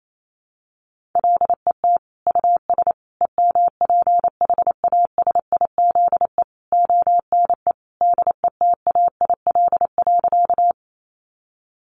Morse code, heard as a keyed tone, about 28 wpm, 700 Hz. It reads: LET US EMPHASIZE ONE DETAIL.
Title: Sentences from Top 1500 Words 28wpm